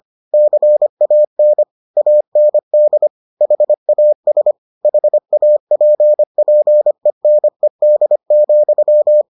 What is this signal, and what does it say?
Morse code, heard as a keyed tone, about 25 wpm, 600 Hz. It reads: CAN AND HAS HAPPENED,